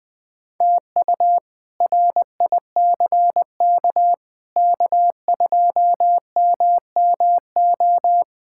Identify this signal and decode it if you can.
Morse code, heard as a keyed tone, about 21 wpm, 700 Hz. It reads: TU RICK K2MMO
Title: K1USN SST QSO Practice - 20wpm